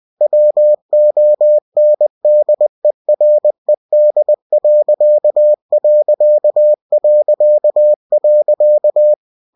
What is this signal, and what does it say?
Morse code, heard as a keyed tone, about 20 wpm, 600 Hz. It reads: WONDERED....
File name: cw-books / winnie-the-pooh / winnie-20wpm-page-0056